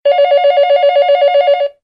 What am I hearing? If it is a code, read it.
There is no code.